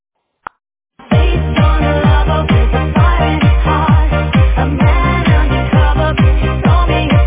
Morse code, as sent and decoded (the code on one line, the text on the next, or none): none